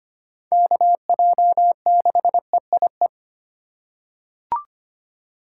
Morse code, heard as a keyed tone, about 25 wpm, 700 Hz. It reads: KJ6EIE
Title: USA Call Signs - 6 Characters - 6x Character Spacing - 25wpm